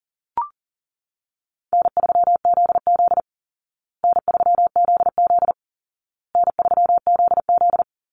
Morse code, heard as a keyed tone, about 40 wpm, 700 Hz. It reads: N3ZZ N3ZZ N3ZZ
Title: Call Signs Repeated 2x - 40wpm